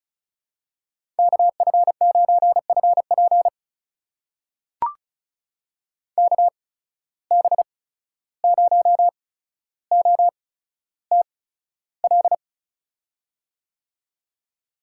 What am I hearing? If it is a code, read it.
Morse code, heard as a keyed tone, about 34 wpm, spaced as for 14 wpm, 700 Hz. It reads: KF9FP KB0OTL
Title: USA Call Signs - 8x Character Spacing - 35wpm